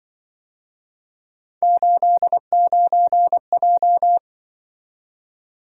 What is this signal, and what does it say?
Morse code, heard as a keyed tone, about 24 wpm, 700 Hz. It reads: 89J
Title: Speed-Racing - Three Letter-Number Combinations - 20wpm